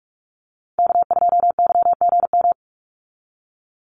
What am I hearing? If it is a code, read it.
Morse code, heard as a keyed tone, about 44 wpm, 700 Hz. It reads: K2YGM
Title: Call Signs Repeated 2x - 45wpm